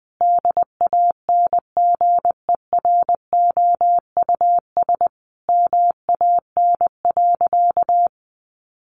Morse code, heard as a keyed tone, about 20 wpm, 700 Hz. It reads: DANGEROUS MAN.
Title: Sentences from Top 1200 Words 20wpm